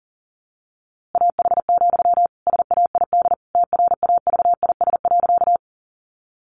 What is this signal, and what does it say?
Morse code, heard as a keyed tone, about 40 wpm, 700 Hz. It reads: AH, SAID TRAVIS.
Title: A Sound Of Thunder 40wpm